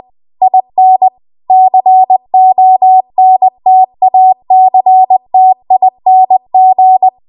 Morse code, heard as a keyed tone, about 20 wpm, 750 Hz. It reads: IN CONTACTING
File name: Archive / 20 WPM / 20WPM